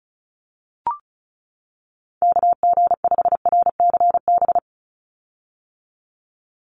Morse code, heard as a keyed tone, about 36 wpm, 700 Hz. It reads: KG5RCB
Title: USA Call Signs 35wpm